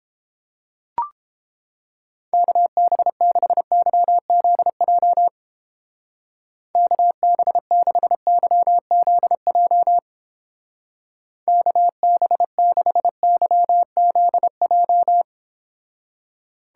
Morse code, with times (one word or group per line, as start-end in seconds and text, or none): 2.3-5.3 KB6YZJ
6.8-10.0 KB6YZJ
11.5-15.2 KB6YZJ